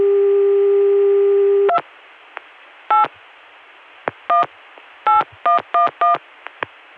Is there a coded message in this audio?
Yes: DTMF from 1.7 to 6.2 s, heard as telephone touch tones, tones of about 140 ms. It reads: 2 8 18111